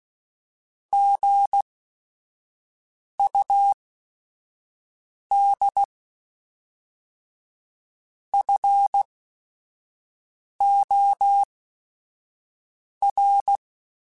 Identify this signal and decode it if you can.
Morse code, heard as a keyed tone, about 16 wpm, spaced as for 5 wpm, 800 Hz. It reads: GUD FOR
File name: QSOs / QSOs-mp3s / WA7QOX 1@5